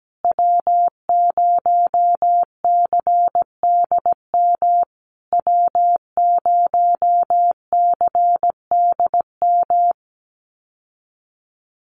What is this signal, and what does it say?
Morse code, heard as a keyed tone, about 17 wpm, 700 Hz. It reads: W0CDM W0CDM
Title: USA Call Signs Repeated 3x 17wpm